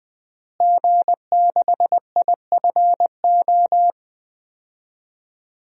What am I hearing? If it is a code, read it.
Morse code, heard as a keyed tone, about 20 wpm, 700 Hz. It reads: G6IFO
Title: DXCC-100 Call Signs - 5 Characters - Repeated 3x - 20wpm